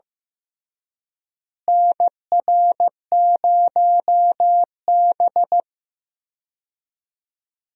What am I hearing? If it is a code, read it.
Morse code, heard as a keyed tone, about 16 wpm, 700 Hz. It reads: NR0B
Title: USA Call Signs - 4 Characters - Repeated 3x - 15wpm